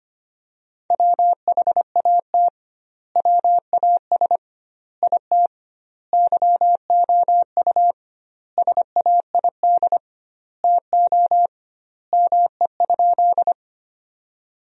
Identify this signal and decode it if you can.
Morse code, heard as a keyed tone, about 25 wpm, 700 Hz. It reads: WHAT WAS IT YOU SAID TO ME?
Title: 2x Word-Spacing - Sentences from Top 200 Words - 25wpm